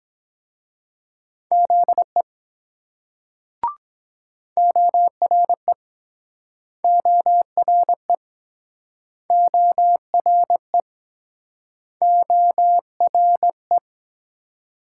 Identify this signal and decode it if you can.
Morse code, heard as a keyed tone, about 22 wpm, 700 Hz. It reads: ZE ORE ORE ORE ORE